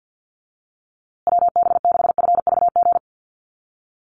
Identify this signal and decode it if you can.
Morse code, heard as a keyed tone, about 48 wpm, 700 Hz. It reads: WB6FVG
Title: USA Call Signs Repeated 3x 50wpm